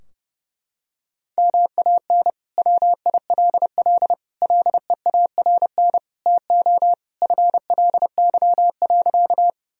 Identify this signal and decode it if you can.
Morse code, heard as a keyed tone, about 30 wpm, 700 Hz. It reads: MAN WILL LEARN TO FLY.
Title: Sentences from Top 1000 Words - Review Part 2 - 30wpm